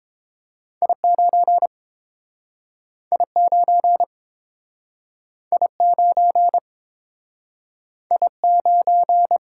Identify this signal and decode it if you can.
Morse code, heard as a keyed tone, about 30 wpm, 700 Hz. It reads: I9 I9 I9 I9